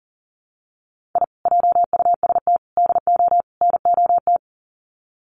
Morse code, heard as a keyed tone, about 40 wpm, 700 Hz. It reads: I JUST DO NOT
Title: Sets of 4 Words 40wpm